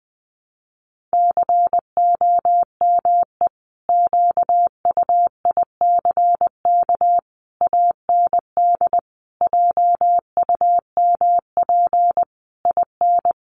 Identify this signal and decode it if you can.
Morse code, heard as a keyed tone, about 20 wpm, 700 Hz. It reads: COME QUICK AND JUMP IN
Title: Sentences from Top 1300 Words 20wpm